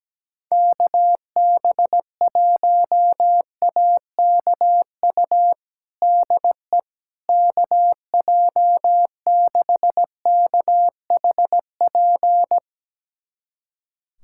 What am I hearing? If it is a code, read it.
Morse code, heard as a keyed tone, about 17 wpm, 700 Hz. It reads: KB1AKU DE KJ6KHP
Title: USA Callsign DE Callsign - 6 Characters - 17wpm